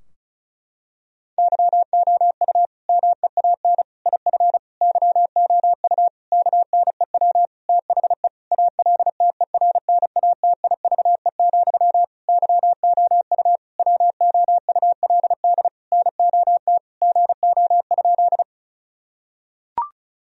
Morse code, heard as a keyed tone, about 36 wpm, 700 Hz. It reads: YOU MEAN IF YOU KNEW THE ALTERNATIVE, YOU WOULD NOT GO?
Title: Sentences from Top 1000 Words - Review Part 4 - 35wpm